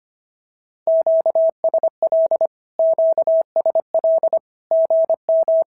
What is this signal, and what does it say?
Morse code, heard as a keyed tone, about 25 wpm, 690 Hz. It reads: QSL QSL GM
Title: SOTA QSO Practice 25wpm